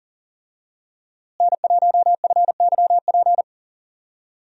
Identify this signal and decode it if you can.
Morse code, heard as a keyed tone, about 40 wpm, 700 Hz. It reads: N1FYP
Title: USA Call Signs - 5 Characters - Repeated 2x - 40wpm